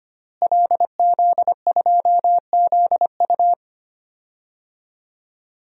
Morse code, heard as a keyed tone, about 25 wpm, 700 Hz. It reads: LZ2ZU
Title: DXCC-100 Call Signs - 5 Characters - Repeated 2x - 25wpm